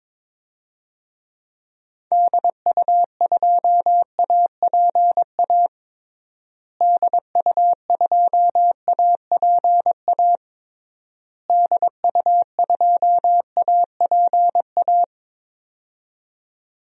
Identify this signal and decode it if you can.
Morse code, heard as a keyed tone, about 21 wpm, 700 Hz. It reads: DU2APA DU2APA DU2APA
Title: DXCC-100 Call Signs - 6 Characters - Repeated 3x - 22wpm